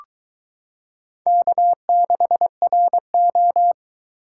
Morse code, heard as a keyed tone, about 22 wpm, 700 Hz. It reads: K6RO